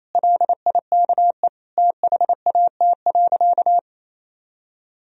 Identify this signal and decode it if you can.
Morse code, heard as a keyed tone, about 28 wpm, 700 Hz. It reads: LIKE THAT.
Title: Sentences from Top 1000 Words - Review Part 3 - 28wpm